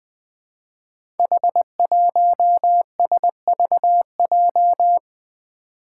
Morse code, heard as a keyed tone, about 20 wpm, 700 Hz. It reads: H1SVJ
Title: Five Challenging Letter-Number Combinations 20wpm